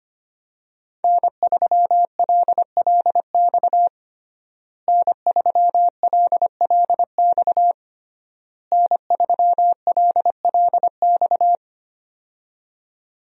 Morse code, heard as a keyed tone, about 25 wpm, 700 Hz. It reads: N3LLX N3LLX N3LLX
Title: USA Call Signs - 5 Characters - Repeated 3x - 25wpm